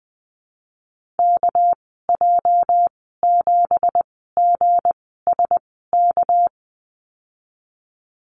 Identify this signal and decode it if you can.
Morse code, heard as a keyed tone, about 20 wpm, spaced as for 14 wpm, 700 Hz. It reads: KJ7GSK
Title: 231 Lesson 37g - Letter x - Call Signs 20wpm